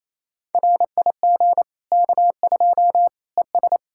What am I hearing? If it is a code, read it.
Morse code, heard as a keyed tone, about 28 wpm, 700 Hz. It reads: RIG K2 ES
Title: Practice QSO - Exchange 2 - Info - 28wpm